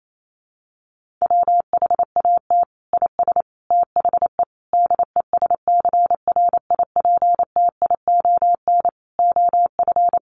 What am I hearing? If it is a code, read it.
Morse code, heard as a keyed tone, about 28 wpm, 700 Hz. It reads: WHAT IS THE DESCRIPTION OF
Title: Sentences from Top 1300 Words 28wpm